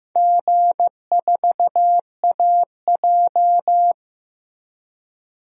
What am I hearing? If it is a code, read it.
Morse code, heard as a keyed tone, about 15 wpm, 700 Hz. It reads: G4AJ
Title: DXCC-100 Call Signs - 4 Characters - Repeated 2x - 15wpm